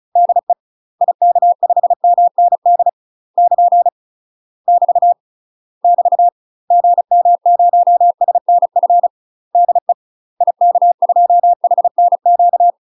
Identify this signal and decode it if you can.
Morse code, heard as a keyed tone, about 36 wpm, 700 Hz. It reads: DE IK5MND <KN> <BT> <BT> GM0SNF DE IK2HNQ
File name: UK MASTER QSOs TRAINING / 35WPM / UK MASTER QSOs 0013